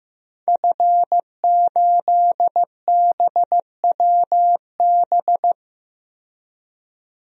Morse code, heard as a keyed tone, about 15 wpm, 700 Hz. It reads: F8BWB
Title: DXCC-100 Call Signs - 5 Characters - 15wpm